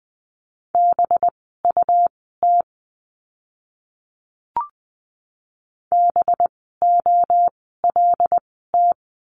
Morse code, heard as a keyed tone, about 20 wpm, spaced as for 15 wpm, 700 Hz. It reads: BUT BOLT
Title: 185 Lesson 30f - Letter b - Words 20wpm